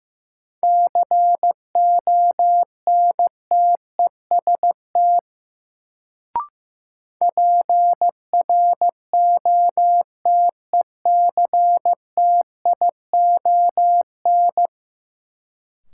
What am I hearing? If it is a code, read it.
Morse code, heard as a keyed tone, about 15 wpm, 700 Hz. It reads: CONTEST PROTECTION
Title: Sentences from Top 1600 Words - New Word Drill Alt - 15wpm